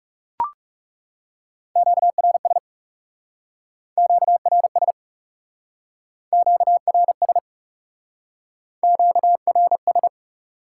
Morse code, heard as a keyed tone, about 39 wpm, 700 Hz. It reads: QRS QRS QRS QRS